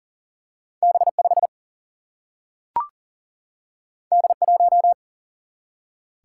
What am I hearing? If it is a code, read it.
Morse code, heard as a keyed tone, about 40 wpm, 700 Hz. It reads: B 5 D 1
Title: Single-Letter Single-Number 40wpm